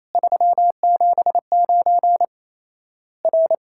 Morse code, heard as a keyed tone, about 28 wpm, 700 Hz. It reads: 379 R
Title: Worked All Europe DX Contest - QTC Exchange - 28wpm